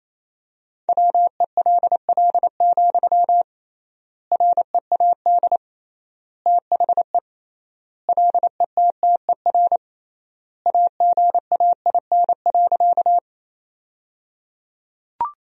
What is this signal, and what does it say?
Morse code, heard as a keyed tone, about 27 wpm, 700 Hz. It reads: WELL, READ THE LETTER AGAIN.
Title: 3x Word-Spacing - Sentences from Top 300 Words - 28wpm